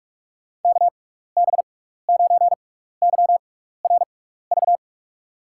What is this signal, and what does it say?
Morse code, heard as a keyed tone, about 46 wpm, spaced as for 15 wpm, 700 Hz. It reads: KB9YRV
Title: USA Call Signs - 6 Characters - 6x Character Spacing - 45wpm